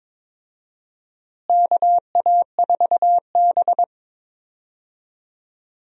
Morse code, heard as a keyed tone, about 22 wpm, 700 Hz. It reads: KA4B
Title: USA Call Signs - 4 Characters - Repeated 3x - 22wpm